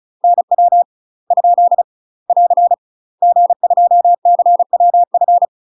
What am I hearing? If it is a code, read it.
Morse code, heard as a keyed tone, about 36 wpm, 700 Hz. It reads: NW ? <AR> G2CWF
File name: UK MASTER QSOs TRAINING / 35WPM / UK MASTER QSOs 0017